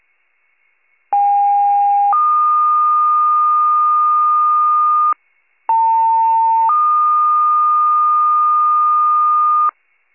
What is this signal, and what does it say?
Two-tone page, heard as a one-second tone, then a three-second tone, about 810 then 1220 Hz, about 900 then 1220 Hz. It is heard from 1.1 to 9.7 s.